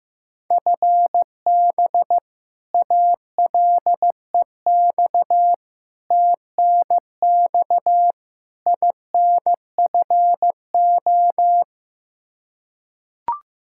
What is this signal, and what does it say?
Morse code, heard as a keyed tone, about 15 wpm, 700 Hz. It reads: FB ALEX TNX INFO
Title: Practice QSO - Exchange 3 - More Info - 15wpm